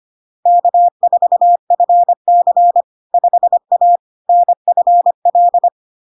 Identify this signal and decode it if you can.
Morse code, heard as a keyed tone, about 25 wpm, 700 Hz. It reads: K4FC 5A NFL
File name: LARK-FD-2022-Log-in-CW-25wpm-700-Hz